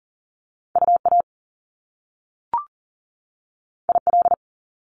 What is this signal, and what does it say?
Morse code, heard as a keyed tone, about 42 wpm, 700 Hz. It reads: UA IL
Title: Top Two Letter Combinations in Words 40wpm